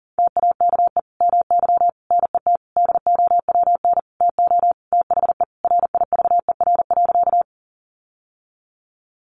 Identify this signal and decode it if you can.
Morse code, heard as a keyed tone, about 40 wpm, 700 Hz. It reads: TAKE MY NET DOWN TO THE RIVER.
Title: Sentences from Top 1400 Words 40wpm